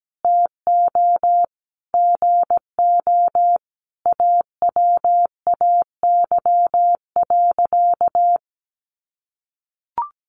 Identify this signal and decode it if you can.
Morse code, heard as a keyed tone, about 17 wpm, 700 Hz. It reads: TO GO AWAY.
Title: Sentences from Top 1500 Words 17wpm